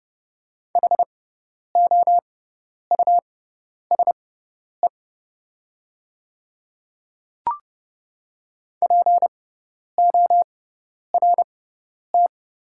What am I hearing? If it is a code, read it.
Morse code, heard as a keyed tone, about 30 wpm, spaced as for 10 wpm, 700 Hz. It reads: HOUSE PORT
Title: 6x Character Spacing - Top 300 Words - 30wpm